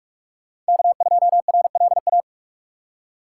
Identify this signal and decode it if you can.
Morse code, heard as a keyed tone, about 44 wpm, 690 Hz. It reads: K2RLA
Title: Code Groups - 5 Letter:Numbers - 45wpm